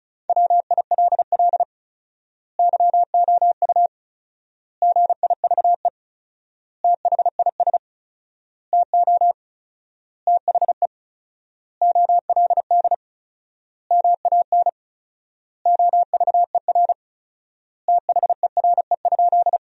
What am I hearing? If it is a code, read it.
Morse code, heard as a keyed tone, about 36 wpm, 700 Hz. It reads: WILL YOU GIVE THIS TO THE OLD MAN OVER THERE?
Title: 4x Word-Spacing - Sentences from Top 200 Words - 35wpm